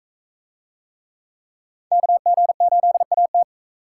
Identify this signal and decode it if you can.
Morse code, heard as a keyed tone, about 44 wpm, 700 Hz. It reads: KG8AT